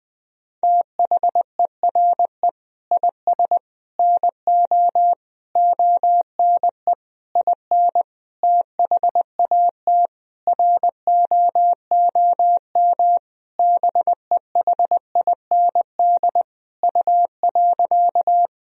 Morse code, heard as a keyed tone, about 20 wpm, 700 Hz. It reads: THERE IS NO ONE IN THAT ROOM BEHIND U.
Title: Sentences from Top 500 Words and Abbreviations 20wpm